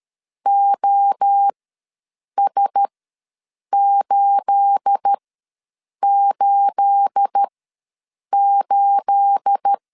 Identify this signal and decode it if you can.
Morse code, heard as a keyed tone, about 13 wpm, spaced as for 7 wpm, 800 Hz. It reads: OS888
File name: LEZIONI / 9 28 ws 12wpm 3354